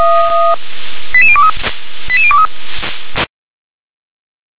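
Five-tone selective call (ZVEI), heard as quick runs of five tones, tones of about 70 ms. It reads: 88023 88023
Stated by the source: ZVEI five-tone selective call received on an SDR.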